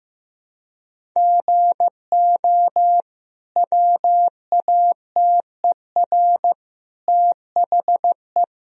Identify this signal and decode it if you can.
Morse code, heard as a keyed tone, about 15 wpm, 700 Hz. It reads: GO WATER THE
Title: Sentences from Top 1000 Words - Review Part 1 - 15wpm